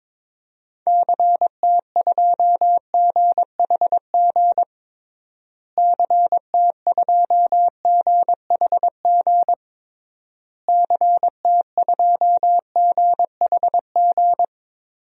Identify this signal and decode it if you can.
Morse code, heard as a keyed tone, about 22 wpm, 700 Hz. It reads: CT2GHG CT2GHG CT2GHG
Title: DXCC-100 Call Signs - 6 Characters - Repeated 3x - 22wpm